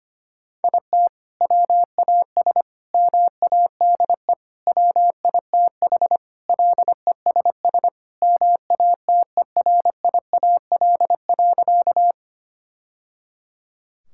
Morse code, heard as a keyed tone, about 25 wpm, 700 Hz. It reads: IT WAS MADE WITH LESS MATERIAL.
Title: Sentences from Top 1000 Words - Review Part 5 - 25wpm